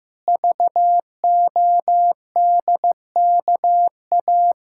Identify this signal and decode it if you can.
Morse code, heard as a keyed tone, about 15 wpm, 700 Hz. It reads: VODKA
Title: Pangram Challenge v2 - 15wpm